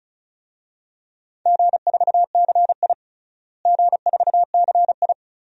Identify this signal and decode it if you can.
Morse code, heard as a keyed tone, about 36 wpm, 700 Hz. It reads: G4CI G4CI
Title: DXCC-100 Call Signs - 4 Characters - Repeated 2x - 35wpm